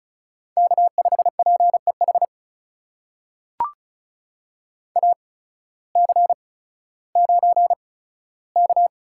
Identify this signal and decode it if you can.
Morse code, heard as a keyed tone, about 34 wpm, spaced as for 16 wpm, 700 Hz. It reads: K5PEH AC9K
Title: USA Call Signs - 8x Character Spacing - 35wpm